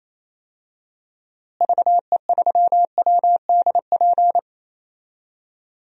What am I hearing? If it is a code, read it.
Morse code, heard as a keyed tone, about 28 wpm, 700 Hz. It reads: VE3WDP
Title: DXCC-100 Call Signs - 6 Characters - Repeated 3x - 28wpm